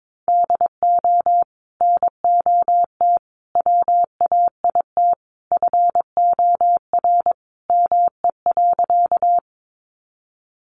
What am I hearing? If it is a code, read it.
Morse code, heard as a keyed tone, about 22 wpm, 700 Hz. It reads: DO NOT WAIT FOR ME.
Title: Sentences from Top 600 Words 22wpm